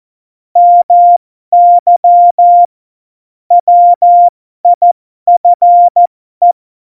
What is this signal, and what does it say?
Morse code, heard as a keyed tone, about 14 wpm, spaced as for 12 wpm, 700 Hz. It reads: MY WIFE
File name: Morsekurs / Witze